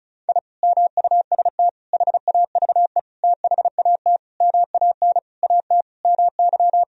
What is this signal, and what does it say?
Morse code, heard as a keyed tone, about 36 wpm, 700 Hz. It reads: I MUST HAVE THAT MAN AT MY
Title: Sentences from Top 1000 Words - Review Part 2 - 35wpm